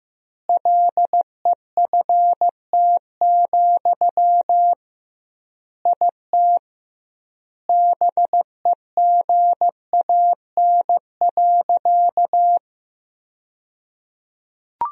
Morse code, heard as a keyed tone, about 15 wpm, 700 Hz. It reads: LEFT, IT BEGAN.
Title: 2x Word-Spacing - Sentences from Top 400 Words - 15wpm